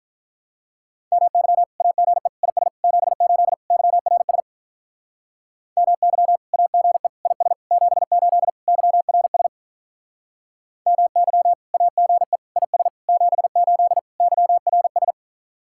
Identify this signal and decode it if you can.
Morse code, heard as a keyed tone, about 44 wpm, 700 Hz. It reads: MY AGE IS 78 YRS MY AGE IS 78 YRS MY AGE IS 78 YRS